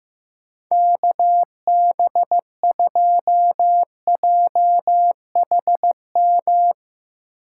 Morse code, heard as a keyed tone, about 15 wpm, 700 Hz. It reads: KB2JHM
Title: USA Call Signs - 4x Character Spacing - 15wpm